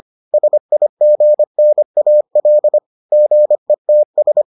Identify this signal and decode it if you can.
Morse code, heard as a keyed tone, about 25 wpm, 600 Hz. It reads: SIGNAL GETS